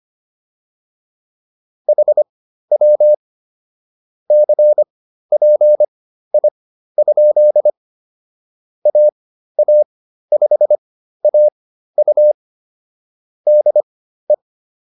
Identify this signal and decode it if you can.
Morse code, heard as a keyed tone, about 25 wpm, spaced as for 13 wpm, 600 Hz. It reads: HW CPI? AA5AU DE
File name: QSO104 13